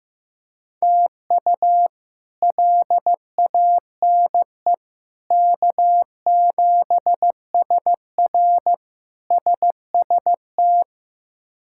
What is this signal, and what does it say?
Morse code, heard as a keyed tone, about 15 wpm, 700 Hz. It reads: TU LANE K7SR SST
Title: K1USN SST QSO Practice - 15wpm